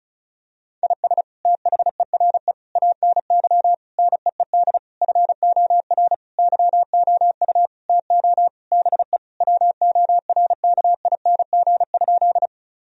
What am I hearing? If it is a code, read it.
Morse code, heard as a keyed tone, about 34 wpm, 700 Hz. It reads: IS THERE ANY NEED FOR YOU TO BE WORKING?
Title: Sentences from Top 1000 Words - Review Part 4 - 35wpm